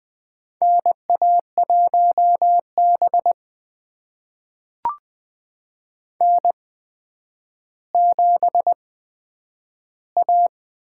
Morse code, heard as a keyed tone, about 20 wpm, spaced as for 10 wpm, 700 Hz. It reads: NA1B N7A
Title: USA Call Signs - 4 Characters - 8x Character Spacing - 20wpm